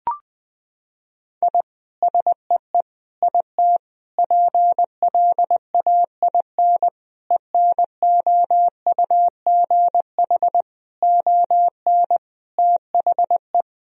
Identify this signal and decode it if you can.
Morse code, heard as a keyed tone, about 20 wpm, 700 Hz. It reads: I SEE IT PLAIN ENOUGH ON THE
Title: Sentences from Top 1000 Words - Review Part 2 - 20wpm